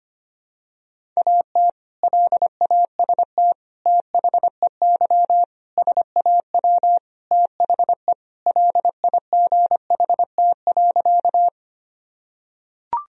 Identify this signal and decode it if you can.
Morse code, heard as a keyed tone, about 25 wpm, 700 Hz. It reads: AT LAST THEY SAW THE LIGHT.
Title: Sentences from Top 1000 Words - Review Part 1 - 25wpm